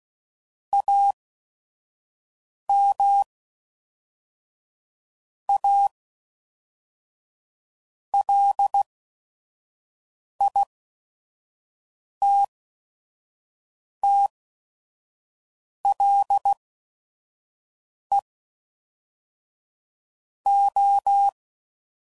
Morse code, heard as a keyed tone, about 16 wpm, spaced as for 5 wpm, 800 Hz. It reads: AM A LITTLE O
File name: CW Code Groups @5WPM / QSOs / kc7haa